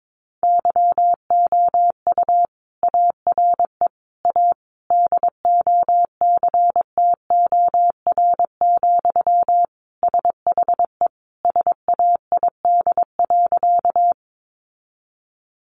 Morse code, heard as a keyed tone, about 22 wpm, 700 Hz. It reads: YOU ARE A DOCTOR, SHE SAID.
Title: Sentences from Top 1500 Words 22wpm